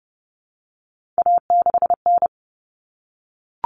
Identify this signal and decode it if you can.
Morse code, heard as a keyed tone, about 30 wpm, 700 Hz. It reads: A6N
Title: All Cut Numbers - 3-Digit 30wpm